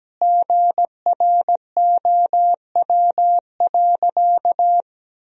Morse code, heard as a keyed tone, about 17 wpm, 700 Hz. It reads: GROW.
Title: Sentences from Top 1000 Words - Review Part 1 - 17wpm